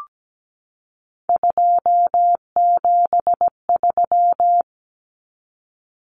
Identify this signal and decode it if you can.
Morse code, heard as a keyed tone, about 17 wpm, 700 Hz. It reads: 273
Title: Common Cut Numbers - 3-Digit 17wpm